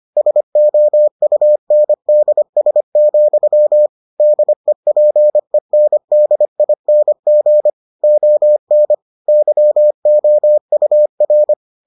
Morse code, heard as a keyed tone, about 25 wpm, 600 Hz. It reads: SOUNDS, DEPENDING ON YOUR